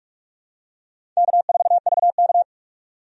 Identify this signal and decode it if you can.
Morse code, heard as a keyed tone, about 46 wpm, 700 Hz. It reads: K4VK
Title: USA Call Signs - 4 Characters - 45wpm